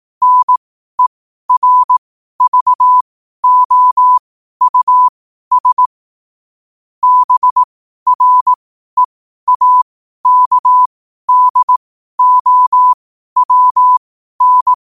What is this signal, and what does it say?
Morse code, heard as a keyed tone, about 18 wpm, spaced as for 12 wpm, 1000 Hz. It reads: NERVOUS BREAKDOWN
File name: qotdfiles / 12wpm